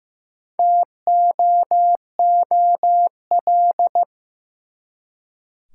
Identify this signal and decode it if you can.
Morse code, heard as a keyed tone, about 15 wpm, 700 Hz. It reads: TOOL
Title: Four Letter Words - 15wpm